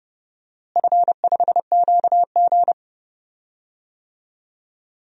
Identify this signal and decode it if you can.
Morse code, heard as a keyed tone, about 32 wpm, 700 Hz. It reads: F5QG
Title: DXCC-100 Call Signs - 4 Characters - Repeated 2x - 30wpm